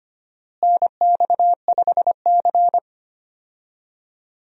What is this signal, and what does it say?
Morse code, heard as a keyed tone, about 25 wpm, 700 Hz. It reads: NX5C
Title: USA Call Signs - 4 Characters - Repeated 2x - 25wpm